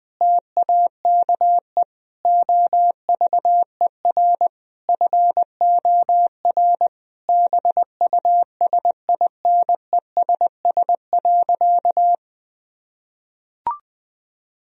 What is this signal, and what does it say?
Morse code, heard as a keyed tone, about 20 wpm, 700 Hz. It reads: TAKE OVER FOR BUSINESS.
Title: Sentences from Top 1000 Words - Review Part 3 - 20wpm